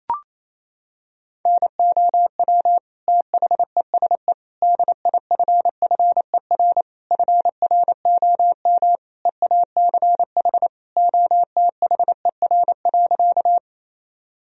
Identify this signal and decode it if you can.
Morse code, heard as a keyed tone, about 27 wpm, 700 Hz. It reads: NOW THESE DIFFER FROM EACH OTHER.
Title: Sentences from Top 1000 Words - Review Part 1 - 28wpm